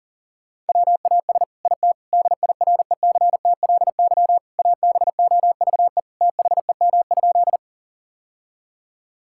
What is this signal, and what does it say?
Morse code, heard as a keyed tone, about 40 wpm, 700 Hz. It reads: WAS IT DIRECTLY ABOVE THEM?
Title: Sentences from Top 1000 Words - Review Part 5 - 40wpm